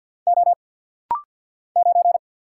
Morse code, heard as a keyed tone, about 49 wpm, 700 Hz. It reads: O 9
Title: Single Letter-Number - Rapid-Fire 50wpm